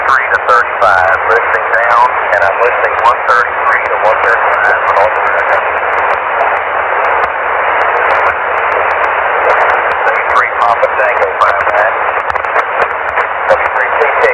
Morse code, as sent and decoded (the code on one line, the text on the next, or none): none